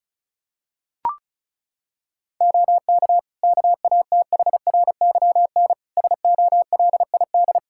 Morse code, heard as a keyed tone, about 34 wpm, 700 Hz. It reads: OK KATHRYN SOLID
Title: K1USN SST QSO Practice 35wpm